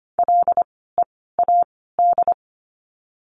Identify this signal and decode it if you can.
Morse code, heard as a keyed tone, about 25 wpm, spaced as for 16 wpm, 700 Hz. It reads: LEAD
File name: course / 25:10wpm / 094 Lesson 16f - Letter c - Words 25wpm